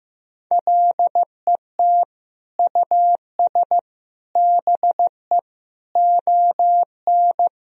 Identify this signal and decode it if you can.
Morse code, heard as a keyed tone, about 15 wpm, 700 Hz. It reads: LET US BE ON
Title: Sentences from Top 1000 Words - Review Part 4 - 15wpm